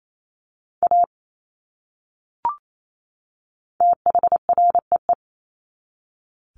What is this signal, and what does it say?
Morse code, heard as a keyed tone, about 28 wpm, 700 Hz. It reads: A THREE
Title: Top 500 Words 28wpm